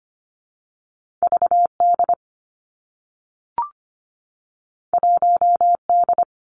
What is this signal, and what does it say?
Morse code, heard as a keyed tone, about 25 wpm, 700 Hz. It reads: VD 1D
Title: Two Challenging Letter-Number Combinations 25wpm